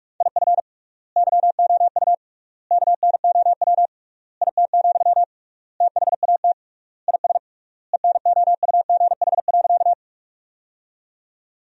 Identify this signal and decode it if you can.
Morse code, heard as a keyed tone, about 46 wpm, 700 Hz. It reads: IF YOU KNOW IT, THAT IS ENOUGH.
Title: 3x Word-Spacing - Sentences from Top 400 Words - 45wpm